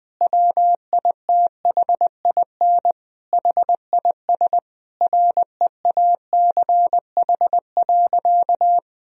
Morse code, heard as a keyed tone, about 21 wpm, 700 Hz. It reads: WITHIN HIS REACH.
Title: Sentences from Top 1000 Words - Review Part 4 - 20wpm